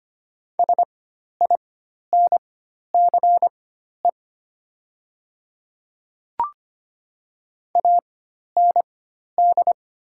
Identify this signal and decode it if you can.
Morse code, heard as a keyed tone, about 25 wpm, spaced as for 12 wpm, 700 Hz. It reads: SINCE AND
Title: 4x Character Spacing - Top 300 Words - 25wpm